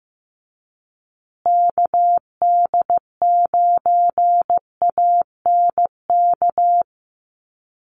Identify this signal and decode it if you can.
Morse code, heard as a keyed tone, about 15 wpm, 700 Hz. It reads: KD9ANK
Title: USA Call Signs Repeated 3x 15wpm